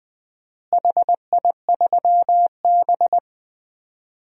Morse code, heard as a keyed tone, about 20 wpm, 700 Hz. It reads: HI3B
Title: DXCC-100 Call Signs - 4 Characters - Repeated 2x - 20wpm